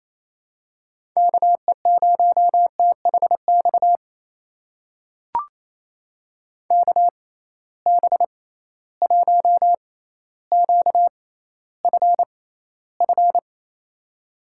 Morse code, heard as a keyed tone, about 28 wpm, spaced as for 14 wpm, 700 Hz. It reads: KE0THX KB1QFF
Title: USA Call Signs - 6 Characters - 6x Character Spacing - 28wpm